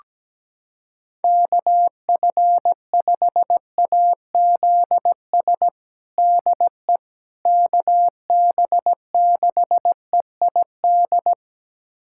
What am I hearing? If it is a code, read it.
Morse code, heard as a keyed tone, about 17 wpm, 700 Hz. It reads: KF5AZS DE KB6EID
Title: USA Callsign DE Callsign - 6 Characters - 17wpm